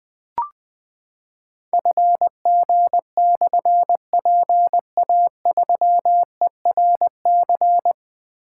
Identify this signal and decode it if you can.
Morse code, heard as a keyed tone, about 21 wpm, 700 Hz. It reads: FG/PA3ERC
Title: Difficult Call Signs 20wpm